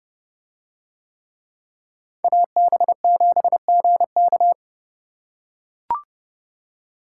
Morse code, heard as a keyed tone, about 28 wpm, 700 Hz. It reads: AB7GK